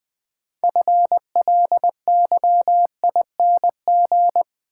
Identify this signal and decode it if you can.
Morse code, heard as a keyed tone, about 20 wpm, 700 Hz. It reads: FLYING
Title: Build A Word - 2 Letters At A Time - 6-Letter Words - 20wpm